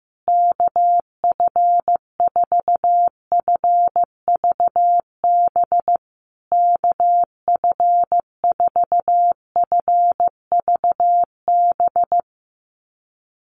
Morse code, heard as a keyed tone, about 15 wpm, 700 Hz. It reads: KF4FVB KF4FVB
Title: USA Call Signs Repeated 3x 15wpm